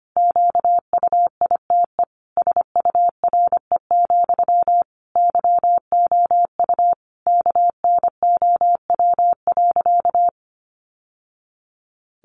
Morse code, heard as a keyed tone, about 25 wpm, 700 Hz. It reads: QUITE SURE, YOU KNOW.
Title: Sentences from Top 700 Words 25wpm